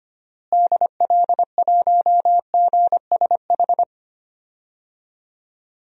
Morse code, heard as a keyed tone, about 25 wpm, 700 Hz. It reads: DL1GSH
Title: DXCC-100 Call Signs - 6 Characters - Repeated 2x - 25wpm